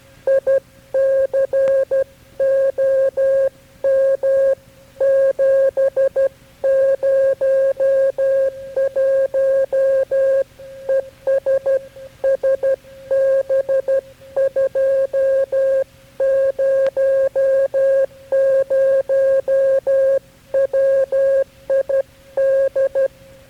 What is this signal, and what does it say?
Morse code, heard as a keyed tone, about 12 wpm, 540 Hz. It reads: ICOM 701 ES SB200 WID
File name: KB6MT-tape3-side1